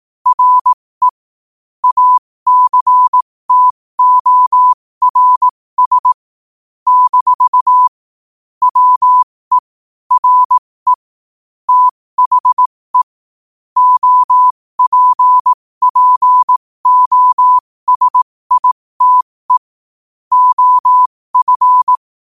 Morse code, heard as a keyed tone, about 18 wpm, spaced as for 15 wpm, 1000 Hz. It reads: RE ACTORS - WE RE THE OPPOSITE OF
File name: qotdfiles / 15wpm